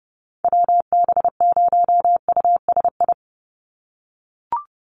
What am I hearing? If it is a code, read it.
Morse code, heard as a keyed tone, about 30 wpm, 700 Hz. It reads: WB0USI
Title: NA SOTA Chaser Callsigns 30wpm